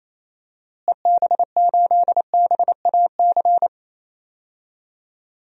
Morse code, heard as a keyed tone, about 28 wpm, 700 Hz. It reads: EB8BAC
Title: DXCC-100 Call Signs - 6 Characters - Repeated 2x - 28wpm